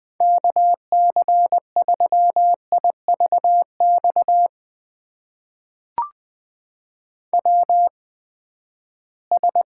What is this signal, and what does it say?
Morse code, heard as a keyed tone, about 20 wpm, spaced as for 14 wpm, 700 Hz. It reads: KC3IVX WS
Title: USA Call Signs - 8x Character Spacing - 20wpm